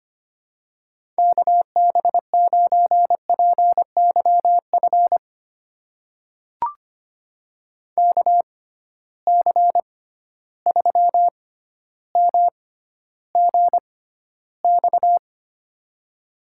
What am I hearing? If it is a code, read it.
Morse code, heard as a keyed tone, about 25 wpm, spaced as for 13 wpm, 700 Hz. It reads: KB9PYF KC3MGX
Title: USA Call Signs - 6 Characters - 6x Character Spacing - 25wpm